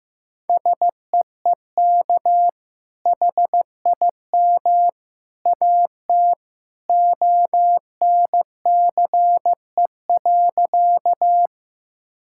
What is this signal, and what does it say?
Morse code, heard as a keyed tone, about 15 wpm, 700 Hz. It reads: SEEK HIM AT ONCE.
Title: Sentences from Top 1600 Words 15wpm